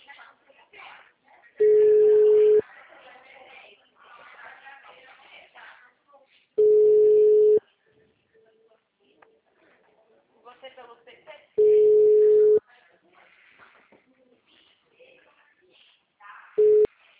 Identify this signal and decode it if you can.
Ringback tone (Europe, 425 Hz), with speech.
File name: out- -2017- - - .40